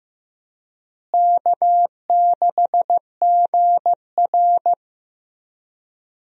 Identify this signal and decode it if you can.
Morse code, heard as a keyed tone, about 15 wpm, 700 Hz. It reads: K6GR
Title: USA Call Signs - 4 Characters - Repeated 3x - 15wpm